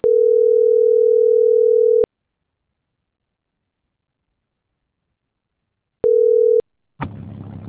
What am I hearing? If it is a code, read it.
Ringback tone (North America), 440 + 480 Hz.
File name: CENTRAL GENERAL / out daniel.valazco